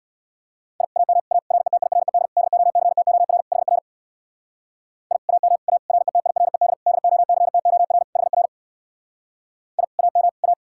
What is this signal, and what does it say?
Morse code, heard as a keyed tone, about 62 wpm, 700 Hz. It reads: I AM A RETIRED DOCTOR HR I AM A RETIRED DOCTOR HR I AM A
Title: Speed-Racing Common QSO Phrases - 50wpm